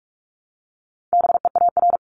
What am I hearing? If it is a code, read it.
Morse code, heard as a keyed tone, about 46 wpm, 700 Hz. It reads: BEAR
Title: Multi-letter Phonemes 45wpm